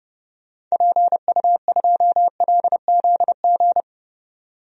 Morse code, heard as a keyed tone, about 30 wpm, 700 Hz. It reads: PU2LZG
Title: DXCC-100 Call Signs - 6 Characters - 30wpm